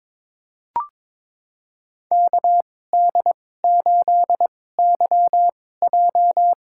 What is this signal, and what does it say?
Morse code, heard as a keyed tone, about 22 wpm, spaced as for 16 wpm, 700 Hz. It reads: KD8YJ
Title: USA Call Signs - 2x Character Spacing - 22wpm